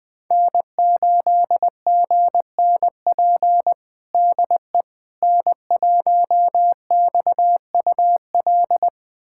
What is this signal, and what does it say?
Morse code, heard as a keyed tone, about 20 wpm, 700 Hz. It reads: N8GNP DE N1XUL
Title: USA Callsign DE Callsign - 5 Characters - 20wpm